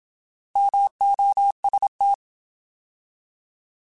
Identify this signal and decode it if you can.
Morse code, heard as a keyed tone, about 27 wpm, 800 Hz. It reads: MOST
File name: Alphabetical / Alphabetical mp3s / Ms @25